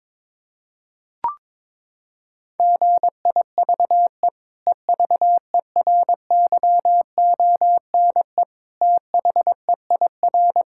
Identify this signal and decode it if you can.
Morse code, heard as a keyed tone, about 22 wpm, 700 Hz. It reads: GIVE EVERYONE THEIR
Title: Sentences from Top 1000 Words - Review Part 6 - 22wpm